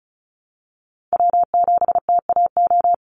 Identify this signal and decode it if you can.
Morse code, heard as a keyed tone, about 34 wpm, 700 Hz. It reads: W7TAO
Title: NA SOTA Chaser Callsigns 35wpm